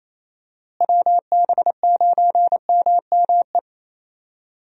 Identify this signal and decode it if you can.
Morse code, heard as a keyed tone, about 28 wpm, 700 Hz. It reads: WB9MME
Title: USA Call Signs - 6 Characters - Repeated 3x - 28wpm